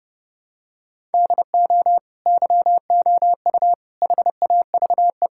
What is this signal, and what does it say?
Morse code, heard as a keyed tone, about 28 wpm, 700 Hz. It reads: DO YOU HAVE
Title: Sentences from Top 1000 Words - Review Part 5 - 30wpm